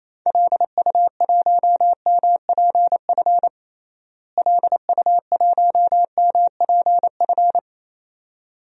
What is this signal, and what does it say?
Morse code, heard as a keyed tone, about 28 wpm, 700 Hz. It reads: LU1MPF LU1MPF
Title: DXCC-100 Call Signs - 6 Characters - Repeated 2x - 28wpm